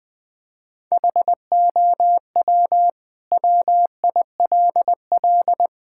Morse code, heard as a keyed tone, about 20 wpm, 700 Hz. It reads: HOW WILL
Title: Sentences from Top 1000 Words - Review Part 5 - 20wpm